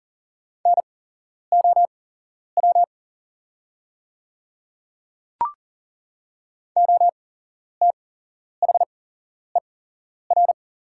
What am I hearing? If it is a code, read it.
Morse code, heard as a keyed tone, about 40 wpm, spaced as for 11 wpm, 690 Hz. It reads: NOW OTHER
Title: 8x Character Spacing - Top 100 Words - 40wpm